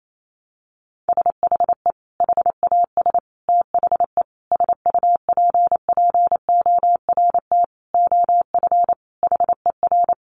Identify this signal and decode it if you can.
Morse code, heard as a keyed tone, about 28 wpm, 700 Hz. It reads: SHE HAS THE SUPPORT OF HER
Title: Sentences from Top 800 Words 28wpm